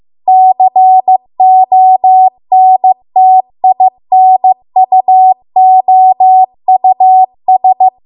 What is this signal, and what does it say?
Morse code, heard as a keyed tone, about 15 wpm, 750 Hz. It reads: CONTINUOUS